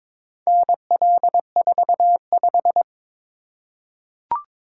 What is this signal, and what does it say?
Morse code, heard as a keyed tone, about 22 wpm, 690 Hz. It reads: NL45
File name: practice / 22wpm / Four Letter-Number Combinations - 22wpm